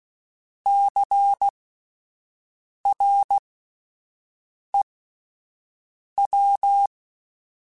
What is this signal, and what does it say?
Morse code, heard as a keyed tone, about 16 wpm, spaced as for 5 wpm, 800 Hz. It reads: CREW
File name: Alphabetical / Alphabetical mp3s / Cs @5